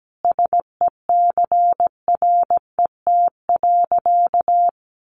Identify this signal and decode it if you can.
Morse code, heard as a keyed tone, about 17 wpm, 700 Hz. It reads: SECRET.
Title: Sentences from Top 1200 Words 17wpm